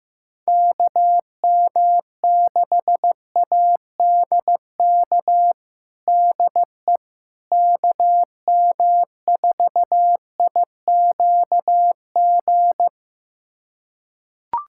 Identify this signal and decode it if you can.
Morse code, heard as a keyed tone, about 16 wpm, 700 Hz. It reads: KM6ADK DE KM4IQG
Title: USA Callsign DE Callsign - 6 Characters - 15wpm